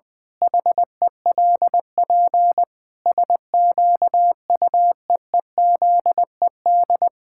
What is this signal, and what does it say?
Morse code, heard as a keyed tone, about 21 wpm, 700 Hz. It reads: HELP SQUEEZED
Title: Pangram Challenge v2 - 20wpm